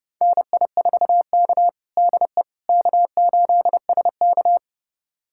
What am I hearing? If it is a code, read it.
Morse code, heard as a keyed tone, about 31 wpm, 700 Hz. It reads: NI4K DE K8SK
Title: USA Callsign DE Callsign - 4 Characters - 30wpm